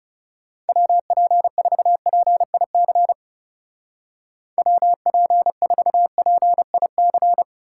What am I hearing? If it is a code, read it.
Morse code, heard as a keyed tone, about 36 wpm, 700 Hz. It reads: WP4PIC WP4PIC
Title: Speed Racing - USA 2x3 Call Signs 30wpm